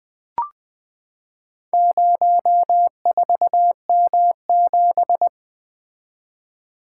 Morse code, heard as a keyed tone, about 20 wpm, 700 Hz. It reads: 04M7
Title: Four Letter-Number Combinations - 20wpm